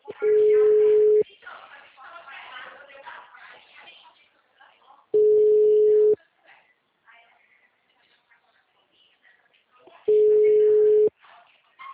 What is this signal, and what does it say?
Ringback tone (Europe, 425 Hz), with speech.
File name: out- -2020- - - .109